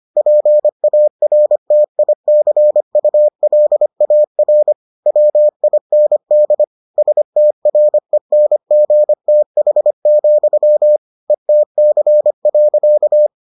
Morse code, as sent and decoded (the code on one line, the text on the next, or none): .--. .- .-. - .. -.-. ..- .-.. .- .-. / .-- .. -. -.. / ... - .-. -. --. - .... --..-- / . - -.-. .-.-.-
PARTICULAR WIND STRENGTH, ETC.